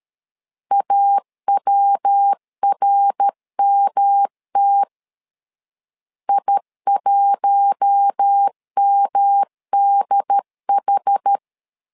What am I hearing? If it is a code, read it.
Morse code, heard as a keyed tone, about 13 wpm, 800 Hz. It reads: AWRMT I1MDH
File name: LEZIONI / 4 Q1 12wpm 2028